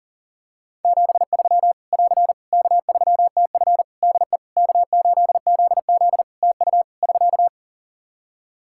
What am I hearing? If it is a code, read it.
Morse code, heard as a keyed tone, about 37 wpm, 700 Hz. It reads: 73 <AR> K3TF DE K8ZZ TU <SK>
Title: K1USN SST QSO Practice 40wpm